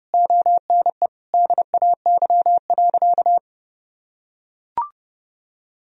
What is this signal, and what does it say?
Morse code, heard as a keyed tone, about 30 wpm, 700 Hz. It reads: ONE DAY.
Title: Sentences from Top 1000 Words - Review Part 1 - 30wpm